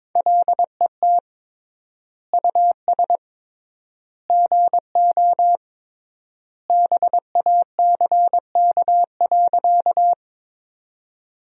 Morse code, heard as a keyed tone, about 22 wpm, 700 Hz. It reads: LET US GO BACK.
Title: 3x Word-Spacing - Sentences from Top 300 Words - 22wpm